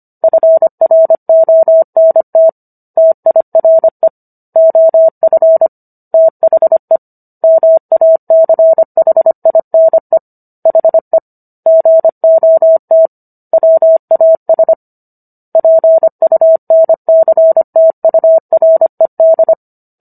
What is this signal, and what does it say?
Morse code, heard as a keyed tone, about 25 wpm, 650 Hz. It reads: FRONT TIRE OF THE MACHINE HE GOT WAS PUNCTURED